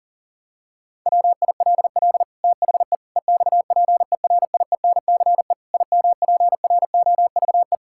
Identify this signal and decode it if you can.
Morse code, heard as a keyed tone, about 40 wpm, 700 Hz. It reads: WILL THE EXPERIENCE IMPROVE
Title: Sentences from Top 1000 Words - Review Part 5 - 40wpm